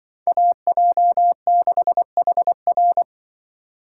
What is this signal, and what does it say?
Morse code, heard as a keyed tone, about 25 wpm, 700 Hz. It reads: AJ6HR
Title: Speed Racing - USA 2x2 Call Signs 20wpm